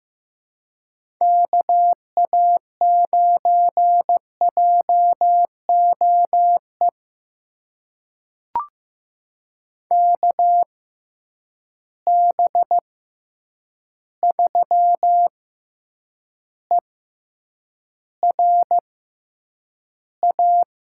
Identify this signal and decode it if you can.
Morse code, heard as a keyed tone, about 15 wpm, spaced as for 8 wpm, 700 Hz. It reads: KA9JOE KB3ERA
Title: USA Call Signs - 6x Character Spacing - 15wpm